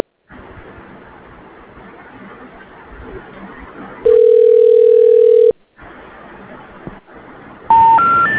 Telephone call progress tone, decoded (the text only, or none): ringback tone